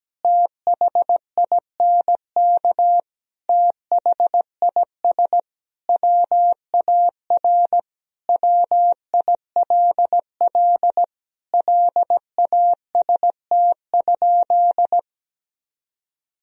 Morse code, heard as a keyed tone, about 17 wpm, 700 Hz. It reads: THINK THIS WAR WILL LAST?
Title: Sentences from Top 1000 Words - Review Part 2 - 17wpm